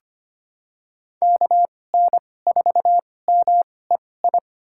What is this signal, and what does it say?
Morse code, heard as a keyed tone, about 25 wpm, spaced as for 18 wpm, 700 Hz. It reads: KN4MEI
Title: USA Call Signs - 6 Characters - 2x Character Spacing - 25wpm